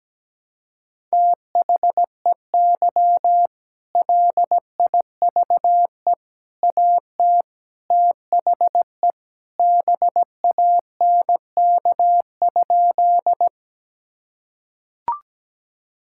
Morse code, heard as a keyed tone, about 17 wpm, 700 Hz. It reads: THEY LIVE AT THE BANK?
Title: Sentences from Top 1000 Words - Review Part 6 - 17wpm